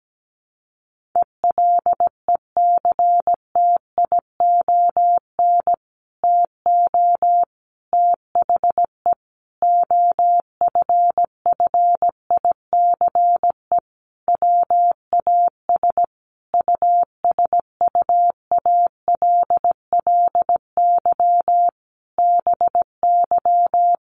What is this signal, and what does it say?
Morse code, heard as a keyed tone, about 17 wpm, 700 Hz. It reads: ELECTION TO THE OFFICE WAS USUALLY BY
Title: Sentences from Top 1500 Words 17wpm